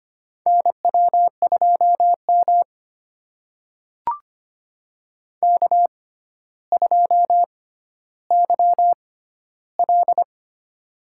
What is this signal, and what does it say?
Morse code, heard as a keyed tone, about 25 wpm, spaced as for 13 wpm, 700 Hz. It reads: NW2M K2YL
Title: USA Call Signs - 4 Characters - 6x Character Spacing - 25wpm